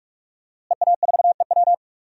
Morse code, heard as a keyed tone, about 43 wpm, 700 Hz. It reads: EA4EW
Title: DXCC-100 Call Signs - 5 Characters - Repeated 2x - 45wpm